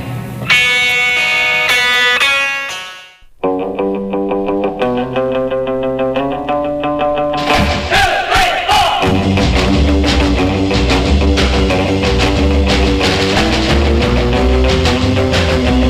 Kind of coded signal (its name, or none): dial-up modem handshake